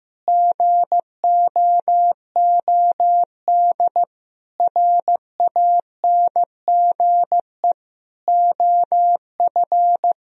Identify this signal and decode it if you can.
Morse code, heard as a keyed tone, about 15 wpm, 700 Hz. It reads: GOOD RANGE OF
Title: Sentences from Top 1000 Words - Review Part 4 - 15wpm